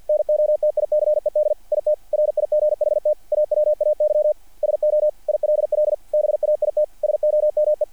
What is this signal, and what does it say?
Morse code, heard as a keyed tone, about 48 wpm, 660 Hz. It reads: NOTICED IT RIGHT AWAY SO ILL BAIT SOME
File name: archived / CRSnet